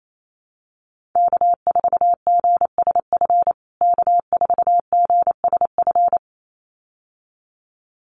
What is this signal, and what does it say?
Morse code, heard as a keyed tone, about 28 wpm, 700 Hz. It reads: K4GSF K4GSF
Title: USA Call Signs Repeated 2x 28wpm